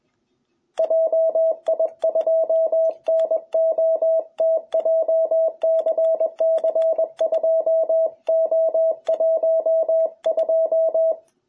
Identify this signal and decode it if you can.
Morse code, heard as a keyed tone, about 20 wpm, 660 Hz. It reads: JI2NOT JCC2O12
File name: jh8cbh / 2021cw